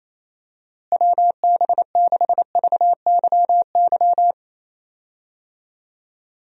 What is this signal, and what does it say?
Morse code, heard as a keyed tone, about 28 wpm, 700 Hz. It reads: WB6VYY
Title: USA Call Signs - 6 Characters - Repeated 2x - 28wpm